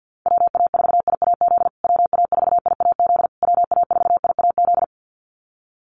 Morse code, heard as a keyed tone, about 50 wpm, 700 Hz. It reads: WA4IAZ WA4IAZ WA4IAZ
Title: USA Call Signs Repeated 3x 50wpm